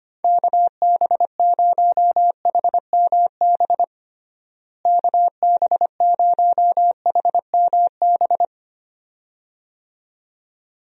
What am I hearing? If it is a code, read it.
Morse code, heard as a keyed tone, about 25 wpm, 700 Hz. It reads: KB0HMB KB0HMB
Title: USA Call Signs - 6 Characters - Repeated 2x - 25wpm